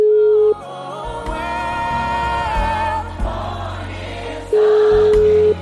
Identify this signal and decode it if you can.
Ringback tone (Europe, 425 Hz).